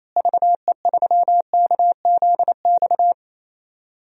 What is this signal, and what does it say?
Morse code, heard as a keyed tone, about 28 wpm, 700 Hz. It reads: VE3KZX
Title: DXCC-100 Call Signs - 6 Characters - Repeated 3x - 28wpm